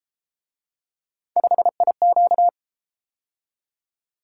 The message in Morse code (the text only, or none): 5IQ